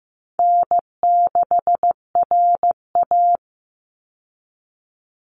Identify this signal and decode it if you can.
Morse code, heard as a keyed tone, about 15 wpm, 700 Hz. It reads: N6RA
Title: Easy Call Signs - 15wpm